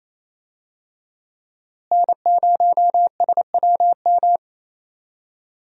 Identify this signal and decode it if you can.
Morse code, heard as a keyed tone, about 28 wpm, 700 Hz. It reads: N0SWM
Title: USA Call Signs - 5 Characters - Repeated 2x - 28wpm